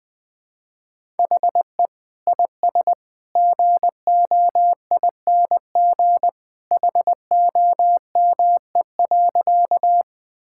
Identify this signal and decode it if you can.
Morse code, heard as a keyed tone, about 20 wpm, 700 Hz. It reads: HE IS GOING HOME.
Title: Sentences with 2-Word Phrases - 20wpm